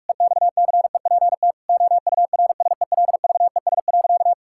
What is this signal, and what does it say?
Morse code, heard as a keyed tone, about 47 wpm, 700 Hz. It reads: EXCEPT OURSELVES.
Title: Sentences from Top 1600 Words 45wpm